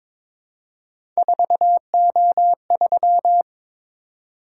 Morse code, heard as a keyed tone, about 22 wpm, 700 Hz. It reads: 4O3
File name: practice / 22wpm / Three Letter-Number Combinations - 22wpm